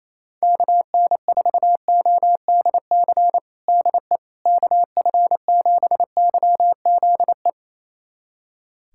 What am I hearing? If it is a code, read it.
Morse code, heard as a keyed tone, about 28 wpm, 700 Hz. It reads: KN4ODC DE KF7YZE
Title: USA Callsign DE Callsign 28wpm